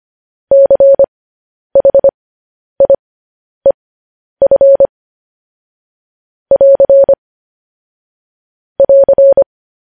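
Morse code, heard as a keyed tone, about 25 wpm, spaced as for 10 wpm, 550 Hz. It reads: CHIEF <AR> <AR>
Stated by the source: Morse code news headlines, likely sent at 10 WPM.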